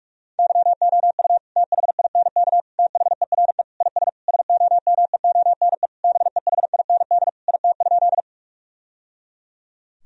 Morse code, heard as a keyed tone, about 42 wpm, 700 Hz. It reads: YOU THINK THERE IS SOMEONE BEHIND IT?
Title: Sentences from Top 1000 Words - Review Part 4 - 45wpm